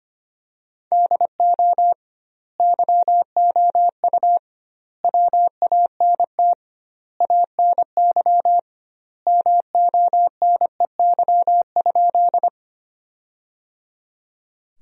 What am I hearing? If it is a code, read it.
Morse code, heard as a keyed tone, about 25 wpm, 700 Hz. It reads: DO YOU WANT ANY MONEY?
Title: 2x Word-Spacing - Sentences from Top 400 Words - 25wpm